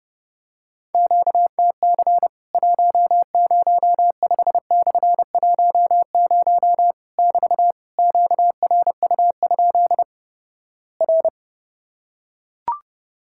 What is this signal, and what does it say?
Morse code, heard as a keyed tone, about 32 wpm, 700 Hz. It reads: QTC 105/10 <BT> QRU? R
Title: Worked All Europe DX Contest - QTC Exchange - 30wpm